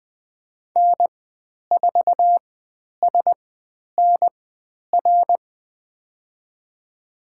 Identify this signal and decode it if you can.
Morse code, heard as a keyed tone, about 20 wpm, spaced as for 10 wpm, 700 Hz. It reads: N4SNR
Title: 102 Lesson 17g - Letter u - Call Signs 20wpm